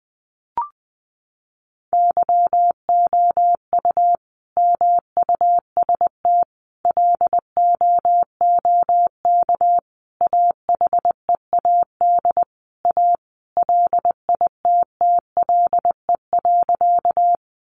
Morse code, heard as a keyed tone, about 20 wpm, 700 Hz. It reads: YOU MUST LOOK AHEAD A LITTLE.
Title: Sentences from Top 1100 Words 20wpm